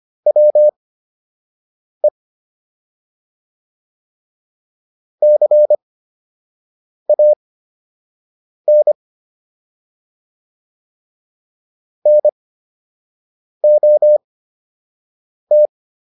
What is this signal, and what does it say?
Morse code, heard as a keyed tone, about 25 wpm, spaced as for 6 wpm, 600 Hz. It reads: WE CAN NOT